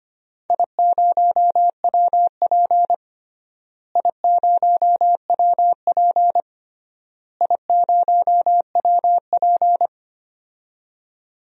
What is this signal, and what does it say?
Morse code, heard as a keyed tone, about 25 wpm, 700 Hz. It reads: I0WP I0WP I0WP
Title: DXCC-100 Call Signs - 4 Characters - Repeated 3x - 25wpm